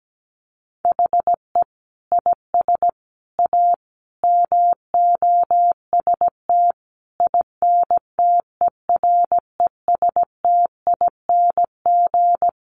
Morse code, heard as a keyed tone, about 17 wpm, 700 Hz. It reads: HE IS A MOST INTERESTING
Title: Sentences from Top 1000 Words 17wpm